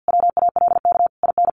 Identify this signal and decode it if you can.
Morse code, heard as a keyed tone, about 50 wpm, 700 Hz. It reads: WALK IN
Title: Sentences from Top 1400 Words 50wpm